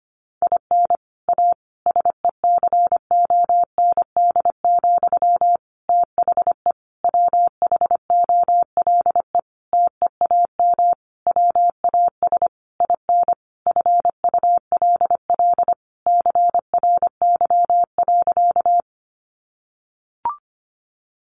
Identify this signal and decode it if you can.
Morse code, heard as a keyed tone, about 25 wpm, 700 Hz. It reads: IN A SECOND, THE WHOLE TEAM WAS IN FULL CRY.
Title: Sentences from Top 1500 Words 25wpm